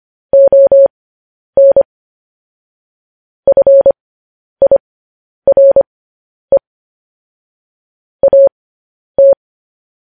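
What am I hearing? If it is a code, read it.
Morse code, heard as a keyed tone, about 25 wpm, spaced as for 10 wpm, 550 Hz. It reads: ON FIRE AT
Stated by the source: Morse code news headlines broadcast.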